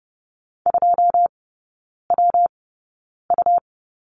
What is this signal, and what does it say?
Morse code, heard as a keyed tone, about 30 wpm, spaced as for 9 wpm, 700 Hz. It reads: 2WU
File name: Lesson 19e - Letter w - Three Characters 30wpm